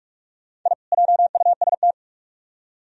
Morse code, heard as a keyed tone, about 40 wpm, 700 Hz. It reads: I JUST